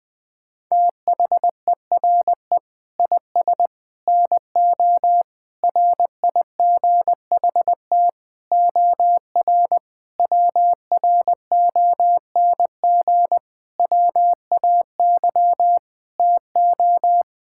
Morse code, heard as a keyed tone, about 21 wpm, 700 Hz. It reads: THERE IS NO RIGHT OR WRONG WAY TO
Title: Sets of 8 Words - 20wpm